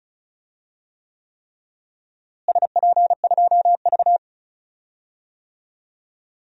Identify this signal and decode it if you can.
Morse code, heard as a keyed tone, about 36 wpm, 700 Hz. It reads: SP2V
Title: DXCC-100 Call Signs - 4 Characters - Repeated 3x - 35wpm